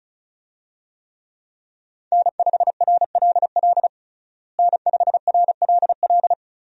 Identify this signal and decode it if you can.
Morse code, heard as a keyed tone, about 36 wpm, 700 Hz. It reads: N5RLL N5RLL
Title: USA Call Signs - 5 Characters - Repeated 2x - 35wpm